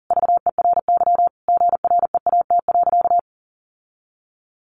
Morse code, heard as a keyed tone, about 40 wpm, 700 Hz. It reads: VERY GREAT.
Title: Sentences from Top 1400 Words 40wpm